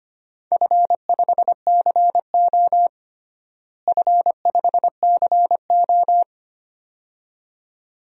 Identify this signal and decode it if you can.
Morse code, heard as a keyed tone, about 25 wpm, 700 Hz. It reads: F5CO F5CO
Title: DXCC-100 Call Signs - 4 Characters - Repeated 2x - 25wpm